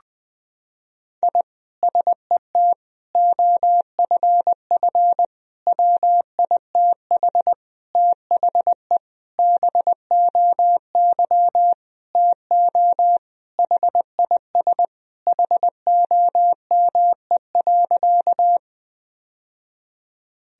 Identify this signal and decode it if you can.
Morse code, heard as a keyed tone, about 20 wpm, 700 Hz. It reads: I SET OFF WITH THE BOY TO HIS HOME.
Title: Sentences from Top 1000 Words - Review Part 1 - 20wpm